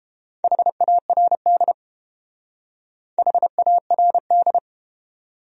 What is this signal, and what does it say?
Morse code, heard as a keyed tone, about 31 wpm, 700 Hz. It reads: HARD HARD